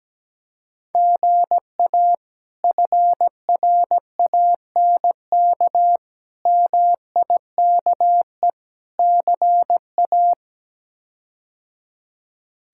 Morse code, heard as a keyed tone, about 17 wpm, 700 Hz. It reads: GA FRANK MIKE CA
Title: K1USN SST QSO Practice - 17wpm